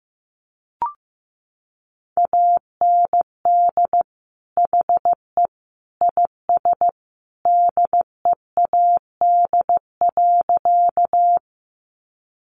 Morse code, heard as a keyed tone, about 15 wpm, 700 Hz. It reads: AND HE IS DEAD.
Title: Sentences from Top 1500 Words 15wpm